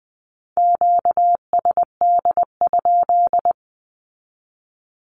Morse code, heard as a keyed tone, about 20 wpm, 700 Hz. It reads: QSD?
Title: Q-Codes with Spoken Meaning - 20wpm